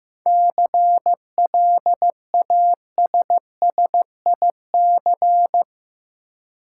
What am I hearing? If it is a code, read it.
Morse code, heard as a keyed tone, about 15 wpm, 700 Hz. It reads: CLASSIC
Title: Seven Letter Words - 15wpm